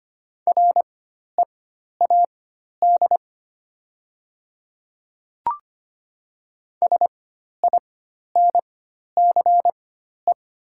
Morse code, heard as a keyed tone, about 25 wpm, spaced as for 12 wpm, 700 Hz. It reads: READ SINCE
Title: 4x Character Spacing - Top 300 Words - 25wpm